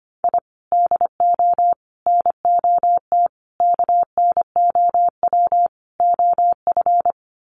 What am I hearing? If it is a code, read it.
Morse code, heard as a keyed tone, about 25 wpm, 700 Hz. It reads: I DO NOT KNOW OF
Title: Sets of 5 Words Encore2 25wpm